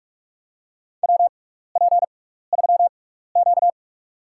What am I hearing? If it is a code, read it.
Morse code, heard as a keyed tone, about 46 wpm, spaced as for 16 wpm, 700 Hz. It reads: WP3Q
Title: USA Call Signs - 4 Characters - 6x Character Spacing - 45wpm